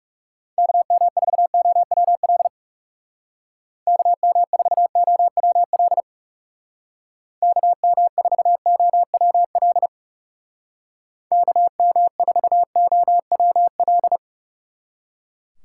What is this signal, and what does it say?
Morse code, heard as a keyed tone, about 36 wpm, 700 Hz. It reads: KM4OWL KM4OWL KM4OWL KM4OWL
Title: Speed Racing - USA Call Signs 30wpm